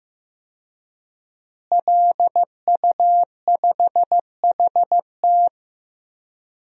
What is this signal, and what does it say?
Morse code, heard as a keyed tone, about 15 wpm, 700 Hz. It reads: LU5HT
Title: DXCC-100 Call Signs - 5 Characters - Repeated 2x - 15wpm